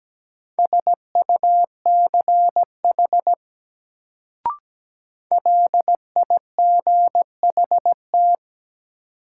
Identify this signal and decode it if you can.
Morse code, heard as a keyed tone, about 17 wpm, 700 Hz. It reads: SUCH LIGHT
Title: Sentences from Top 200 Words - New Word Drill - 17wpm